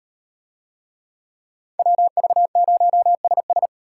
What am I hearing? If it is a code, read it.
Morse code, heard as a keyed tone, about 38 wpm, 700 Hz. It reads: WV0SS